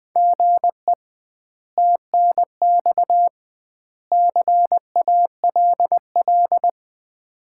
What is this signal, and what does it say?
Morse code, heard as a keyed tone, about 20 wpm, 700 Hz. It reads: GE TNX CALL
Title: Common QSO Phrases - 2x Word-Spacing - 20wpm